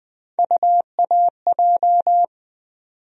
Morse code, heard as a keyed tone, about 20 wpm, 700 Hz. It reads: UAJ
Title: USA Call Sign Postfixes - 3 Letters 20wpm